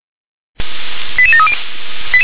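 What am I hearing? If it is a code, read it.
Five-tone selective call (ZVEI), heard as quick runs of five tones, tones of about 70 ms. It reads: 88730
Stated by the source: ZVEI five-tone call received via SDR.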